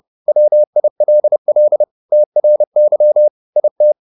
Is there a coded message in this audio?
Yes: Morse code from 0.3 to 3.9 s, heard as a keyed tone, about 30 wpm, 600 Hz. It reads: WILL TRY IT